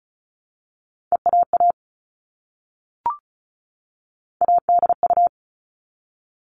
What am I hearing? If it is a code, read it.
Morse code, heard as a keyed tone, about 34 wpm, 700 Hz. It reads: EAA ADU
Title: All Cut Numbers - 3-Digit 35wpm